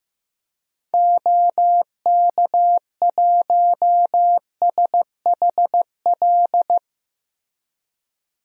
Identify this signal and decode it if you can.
Morse code, heard as a keyed tone, about 15 wpm, 700 Hz. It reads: OK1SHL
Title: DXCC-100 Call Signs - 6 Characters - Repeated 3x - 15wpm